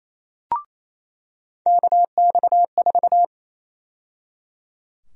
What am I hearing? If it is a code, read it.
Morse code, heard as a keyed tone, about 28 wpm, 700 Hz. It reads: KX4
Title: USA Call Sign Prefixes - 28wpm